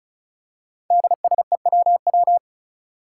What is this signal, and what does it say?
Morse code, heard as a keyed tone, about 36 wpm, 700 Hz. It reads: DSEWW
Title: Five Challenging Letter-Number Combinations 35wpm